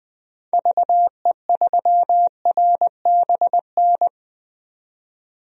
Morse code, heard as a keyed tone, about 21 wpm, 700 Hz. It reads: VE3RBN
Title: DXCC-100 Call Signs - 6 Characters - 20wpm